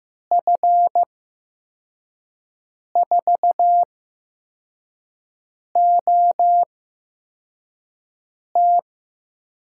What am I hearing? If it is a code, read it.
Morse code, heard as a keyed tone, about 15 wpm, spaced as for 4 wpm, 700 Hz. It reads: F4OT
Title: USA Call Signs - 6 Characters - 8x Character Spacing - 15wpm